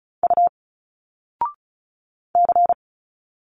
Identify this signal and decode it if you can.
Morse code, heard as a keyed tone, about 37 wpm, 700 Hz. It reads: U C
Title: Single Letters 35wpm